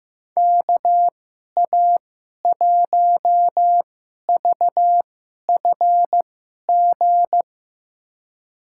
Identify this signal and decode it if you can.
Morse code, heard as a keyed tone, about 15 wpm, spaced as for 11 wpm, 700 Hz. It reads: KA1VFG
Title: USA Call Signs - 6 Characters - 2x Character Spacing - 15wpm